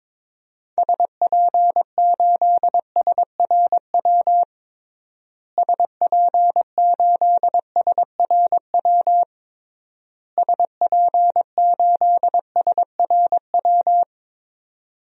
Morse code, heard as a keyed tone, about 21 wpm, 700 Hz. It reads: SP8SRW SP8SRW SP8SRW
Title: DXCC-100 Call Signs - 6 Characters - Repeated 3x - 22wpm